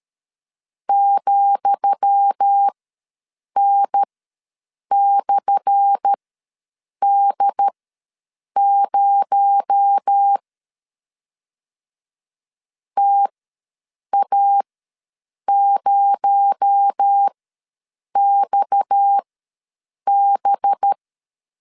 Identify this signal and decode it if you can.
Morse code, heard as a keyed tone, about 13 wpm, spaced as for 7 wpm, 800 Hz. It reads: ,N/D0 TA0XB
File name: LEZIONI / 10 SIGNS / 10D COMA ws 12wpm 3618